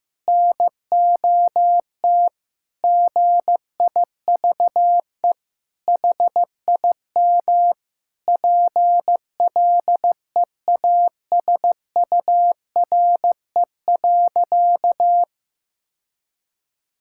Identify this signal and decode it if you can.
Morse code, heard as a keyed tone, about 15 wpm, 700 Hz. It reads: NOT GIVE HIM PLEASURE.
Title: Sentences from Top 1600 Words 15wpm